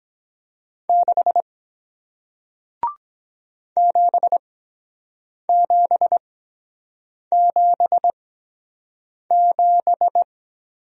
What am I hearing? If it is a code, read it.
Morse code, heard as a keyed tone, about 23 wpm, spaced as for 7 wpm, 700 Hz. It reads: 6 7777